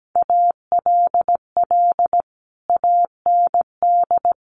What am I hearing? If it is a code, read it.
Morse code, heard as a keyed tone, about 17 wpm, 700 Hz. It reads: ALL AND
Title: A Sound Of Thunder 17wpm